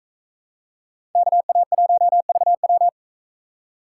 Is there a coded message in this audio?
Yes: Morse code from 1.1 to 2.9 s, heard as a keyed tone, about 40 wpm, 700 Hz. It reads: KA1VW